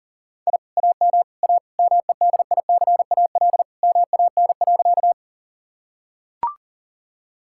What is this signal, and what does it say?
Morse code, heard as a keyed tone, about 40 wpm, 700 Hz. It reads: I AM A MEDICAL MAN.
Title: Sentences from Top 1000 Words - Review Part 5 - 40wpm